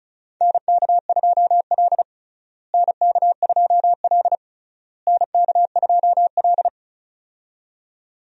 Morse code, heard as a keyed tone, about 34 wpm, 700 Hz. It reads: NK2L NK2L NK2L
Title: USA Call Signs - 4 Characters - Repeated 3x - 35wpm